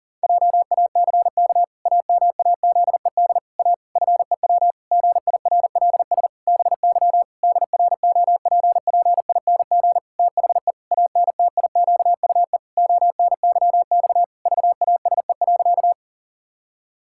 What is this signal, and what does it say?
Morse code, heard as a keyed tone, about 39 wpm, 700 Hz. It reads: JACK AMAZED A FEW GIRLS BY DROPPING THE ANTIQUE ONYX VASE.
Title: Pangram Challenge v2 - 40wpm